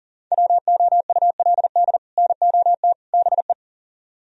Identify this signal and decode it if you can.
Morse code, heard as a keyed tone, about 52 wpm, 700 Hz. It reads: WOULD NOT BE